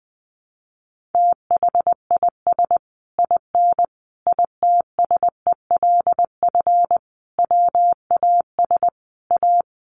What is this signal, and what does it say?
Morse code, heard as a keyed tone, about 20 wpm, 700 Hz. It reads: THIS IN ITSELF WAS A
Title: Sentences from Top 1400 Words 20wpm